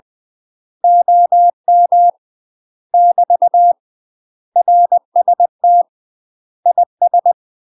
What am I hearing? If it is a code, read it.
Morse code, heard as a keyed tone, about 20 wpm, 700 Hz. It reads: OM <BT> RST IS
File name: UK MASTER QSOs TRAINING / 20WPM / UK MASTER QSOs 0040